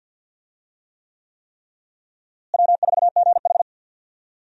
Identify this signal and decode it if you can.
Morse code, heard as a keyed tone, about 53 wpm, 700 Hz. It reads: W4GH
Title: USA Call Signs - 4 Characters - Repeated 3x - 50wpm